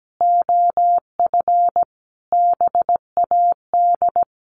Morse code, heard as a keyed tone, about 17 wpm, 700 Hz. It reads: OF BAD
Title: Sentences from Top 1200 Words 17wpm